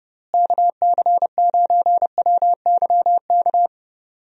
Morse code, heard as a keyed tone, about 30 wpm, 700 Hz. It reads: KC9WYK
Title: USA Call Signs - 6 Characters - Repeated 2x - 30wpm